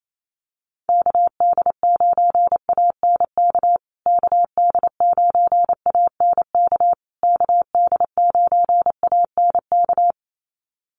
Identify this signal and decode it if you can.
Morse code, heard as a keyed tone, about 28 wpm, 700 Hz. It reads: KD9ANK KD9ANK KD9ANK
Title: USA Call Signs Repeated 3x 28wpm